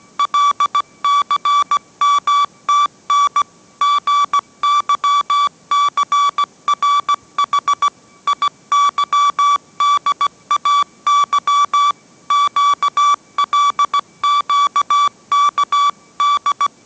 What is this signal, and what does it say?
Morse code, heard as a keyed tone, about 17 wpm, 1150 Hz. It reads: LCMTN GYCRH IYDAY QLQKD